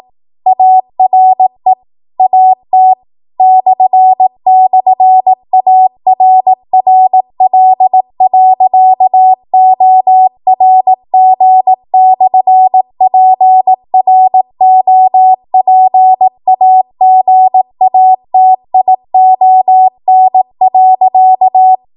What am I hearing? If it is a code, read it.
Morse code, heard as a keyed tone, about 18 wpm, 750 Hz. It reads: ARE AT //ARRL.ORG/PROPAGATION.
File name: Archive / 18 WPM / 18WPM